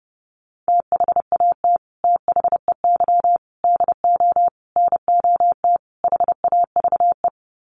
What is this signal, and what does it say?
Morse code, heard as a keyed tone, about 30 wpm, 700 Hz. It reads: THAT THEY DO NOT HAVE
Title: Sets of 5 Words Encore2 30wpm